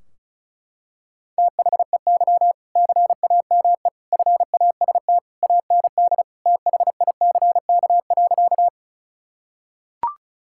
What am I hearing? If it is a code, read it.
Morse code, heard as a keyed tone, about 34 wpm, 700 Hz. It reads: THEY CAME FAST AND THICK.
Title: Sentences from Top 1600 Words 35wpm